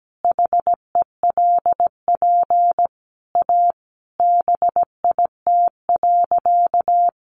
Morse code, heard as a keyed tone, about 17 wpm, 700 Hz. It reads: HELP A BIT.
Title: Sentences from Top 900 Words 17wpm